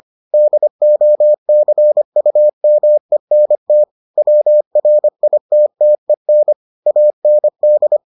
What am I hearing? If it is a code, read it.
Morse code, heard as a keyed tone, about 25 wpm, 600 Hz. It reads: DOCUMENT WRITTEN AND